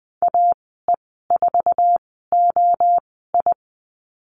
Morse code, heard as a keyed tone, about 20 wpm, spaced as for 15 wpm, 700 Hz. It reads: AE4OI
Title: 076 Lesson 13g - Letter L - Call Signs 20wpm